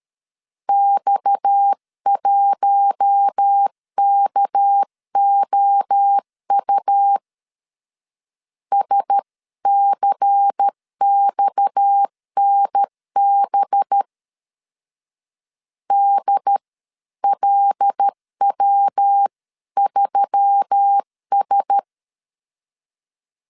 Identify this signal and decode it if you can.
Morse code, heard as a keyed tone, about 13 wpm, 800 Hz. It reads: X1KOU SCXNB DLW3S
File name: CW / DET / 1 6 1500C S C 12wpm 3533